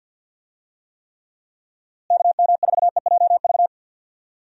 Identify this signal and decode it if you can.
Morse code, heard as a keyed tone, about 51 wpm, 700 Hz. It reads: KM4EJV
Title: USA Call Signs - 6 Characters - Repeated 2x - 50wpm